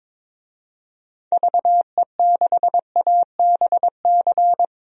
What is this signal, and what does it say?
Morse code, heard as a keyed tone, about 22 wpm, 700 Hz. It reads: VE6ABC
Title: DXCC-100 Call Signs - 6 Characters - 22wpm